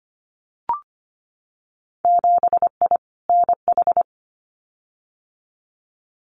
Morse code, heard as a keyed tone, about 25 wpm, 700 Hz. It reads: 7I NH
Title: Winter Field Day Exchanges Encore 25wpm